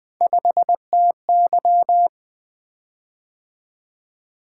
Morse code, heard as a keyed tone, about 20 wpm, 700 Hz. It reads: 5TY
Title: Three Letter-Number Combinations - 20wpm